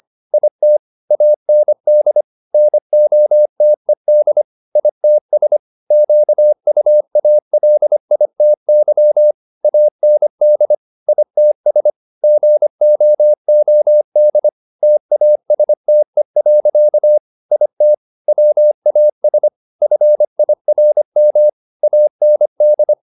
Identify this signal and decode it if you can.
Morse code, heard as a keyed tone, about 25 wpm, 600 Hz. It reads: IT AND NOTED ITS QUALITY AND ITS GOOD TASTE. IT WAS FIRM AND